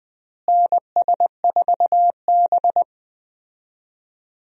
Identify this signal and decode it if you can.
Morse code, heard as a keyed tone, about 20 wpm, 700 Hz. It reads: NS4B
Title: USA Call Signs - 4 Characters - Repeated 3x - 20wpm